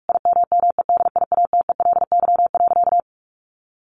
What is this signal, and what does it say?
Morse code, heard as a keyed tone, about 44 wpm, 700 Hz. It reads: IMMEDIATELY.
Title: Sentences from Top 1100 Words 45wpm